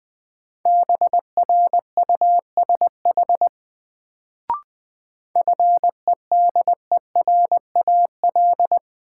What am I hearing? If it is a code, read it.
Morse code, heard as a keyed tone, about 21 wpm, 700 Hz. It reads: BRUSH FEDERAL
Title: Sentences from Top 1100 Words - New Word Drill Alt - 20wpm